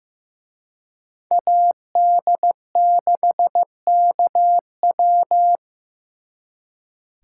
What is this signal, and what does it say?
Morse code, heard as a keyed tone, about 18 wpm, 700 Hz. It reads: AD6KW
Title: Speed Racing - USA 2x2 Call Signs 15wpm